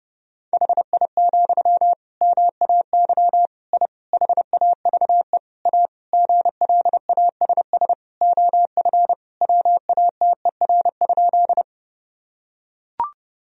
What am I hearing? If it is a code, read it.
Morse code, heard as a keyed tone, about 30 wpm, 700 Hz. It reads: HI, MAY I HAVE A GLASS OF WATER?
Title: Sentences from Top 1000 Words - Review Part 3 - 30wpm